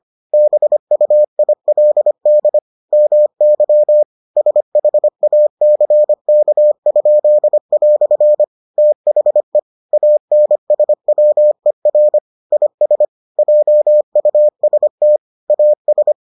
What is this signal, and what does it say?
Morse code, heard as a keyed tone, about 25 wpm, 600 Hz. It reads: BUILD MY SHACK?" THE ANSWER IS JUST AS